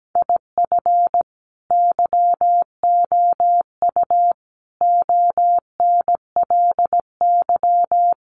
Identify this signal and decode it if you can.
Morse code, heard as a keyed tone, about 17 wpm, 700 Hz. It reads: IF YOU ONLY
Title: Sentences from Top 1500 Words 17wpm